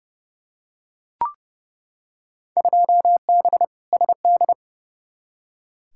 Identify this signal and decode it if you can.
Morse code, heard as a keyed tone, about 32 wpm, 700 Hz. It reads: 2B SD
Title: Field Day Exchanges Encore - 30wpm